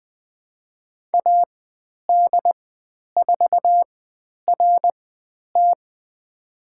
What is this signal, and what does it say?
Morse code, heard as a keyed tone, about 20 wpm, spaced as for 10 wpm, 700 Hz. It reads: AD4RT
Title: 095 Lesson 16g - Letter c - Call Signs 20wpm